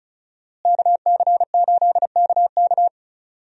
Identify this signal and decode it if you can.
Morse code, heard as a keyed tone, about 36 wpm, 700 Hz. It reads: KC8KK
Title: USA Call Signs - 5 Characters - Repeated 2x - 35wpm